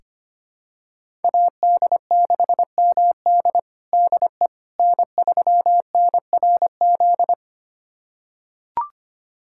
Morse code, heard as a keyed tone, about 25 wpm, 700 Hz. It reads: AD6MD DE N3NRZ
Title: USA Callsign DE Callsign - 5 Characters - 25wpm